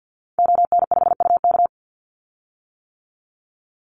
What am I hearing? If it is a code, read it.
Morse code, heard as a keyed tone, about 50 wpm, 700 Hz. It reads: ON5UK
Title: Call Signs - 50wpm